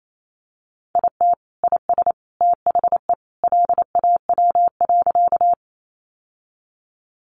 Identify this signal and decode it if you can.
Morse code, heard as a keyed tone, about 28 wpm, 700 Hz. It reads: IT IS THE LAW.
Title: Sentences from Top 600 Words 28wpm